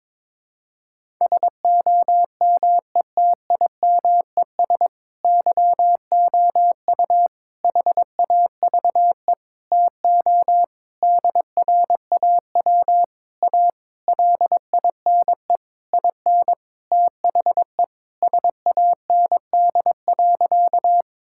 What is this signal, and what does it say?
Morse code, heard as a keyed tone, about 22 wpm, 700 Hz. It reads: SOMETIMES YOU HAVE TO DRAW A LINE IN THE SAND.
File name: Sentences with 7-Word Phrases - 22wpm